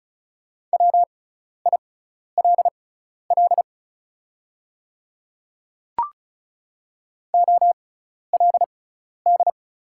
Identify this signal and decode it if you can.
Morse code, heard as a keyed tone, about 34 wpm, spaced as for 12 wpm, 690 Hz. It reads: WILL OLD
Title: 6x Character Spacing - Top 200 Words - 35wpm